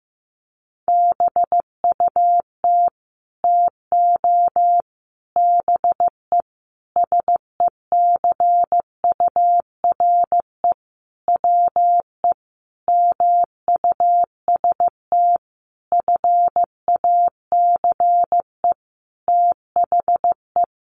Morse code, heard as a keyed tone, about 15 wpm, 700 Hz. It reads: BUT TO BE SECURE WE MUST FACE THE
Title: Sentences from Top 1200 Words 15wpm